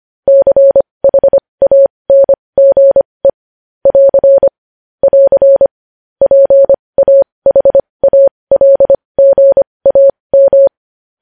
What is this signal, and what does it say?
Morse code, heard as a keyed tone, about 25 wpm, spaced as for 20 wpm, 550 Hz. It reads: CHANGE <AR> <AR> PAHALGAM
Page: news headlines in morse code at 20 wpm